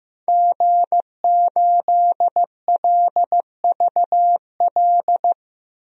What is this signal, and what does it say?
Morse code, heard as a keyed tone, about 15 wpm, 700 Hz. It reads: G8LVL
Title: DXCC-100 Call Signs - 5 Characters - Repeated 3x - 15wpm